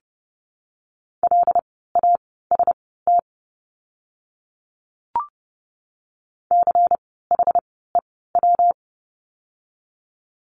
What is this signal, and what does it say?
Morse code, heard as a keyed tone, about 30 wpm, spaced as for 17 wpm, 700 Hz. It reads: LAST CHEW
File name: course / 30:10wpm / 115 Lesson 19f - Letter w - Words 30wpm